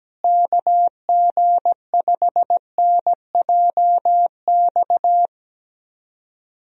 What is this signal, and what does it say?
Morse code, heard as a keyed tone, about 17 wpm, 700 Hz. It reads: KG5NJX
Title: USA Call Signs - 6 Characters - Repeated 2x - 17wpm